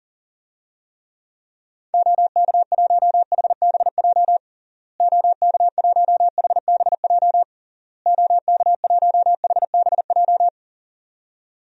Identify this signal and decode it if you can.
Morse code, heard as a keyed tone, about 40 wpm, 700 Hz. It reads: OK1HBJ OK1HBJ OK1HBJ
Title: DXCC-100 Call Signs - 6 Characters - Repeated 3x - 40wpm